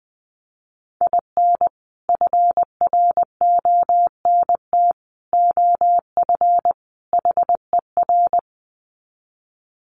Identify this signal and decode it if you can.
Morse code, heard as a keyed tone, about 20 wpm, 700 Hz. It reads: IN FRONT OF HER
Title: Sets of 4 Words 20wpm